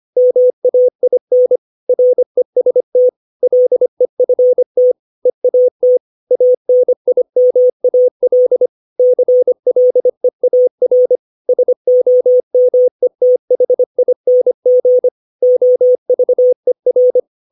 Morse code, heard as a keyed tone, about 25 wpm, 500 Hz. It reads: MAIN REST LEFT EAT ANIMAL CLEAR SOMETHING OVER
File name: 632 words / 632 Words 25WPM